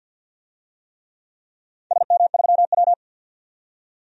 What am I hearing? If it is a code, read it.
Morse code, heard as a keyed tone, about 46 wpm, 700 Hz. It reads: SM3W
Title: DXCC-100 Call Signs - 4 Characters - 50wpm